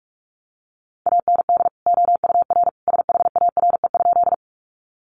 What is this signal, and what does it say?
Morse code, heard as a keyed tone, about 44 wpm, 700 Hz. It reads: AND OUR SHARE?
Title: Sentences from Top 900 Words 45wpm